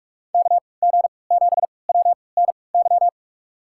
Morse code, heard as a keyed tone, about 46 wpm, spaced as for 26 wpm, 700 Hz. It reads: KG7WNY
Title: USA Call Signs - 6 Characters - 3x Character Spacing - 45wpm